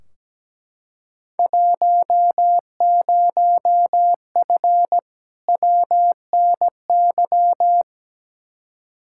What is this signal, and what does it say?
Morse code, heard as a keyed tone, about 17 wpm, 700 Hz. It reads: 10F WNY
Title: Field Day Exchanges Encore - 17wpm